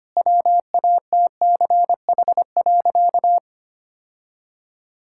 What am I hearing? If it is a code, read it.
Morse code, heard as a keyed tone, about 25 wpm, 700 Hz. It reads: WATCH.
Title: Sentences from Top 1600 Words 25wpm